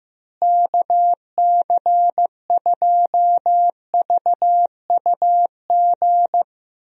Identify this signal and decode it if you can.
Morse code, heard as a keyed tone, about 15 wpm, 700 Hz. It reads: KC2VUG
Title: USA Call Signs - 6 Characters - Repeated 3x - 15wpm